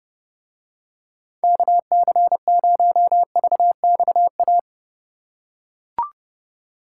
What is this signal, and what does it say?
Morse code, heard as a keyed tone, about 30 wpm, 700 Hz. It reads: KC0VXA